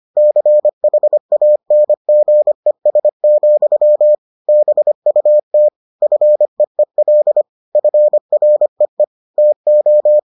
Morse code, heard as a keyed tone, about 25 wpm, 600 Hz. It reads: CHANGES, BUT FEEL FREE TO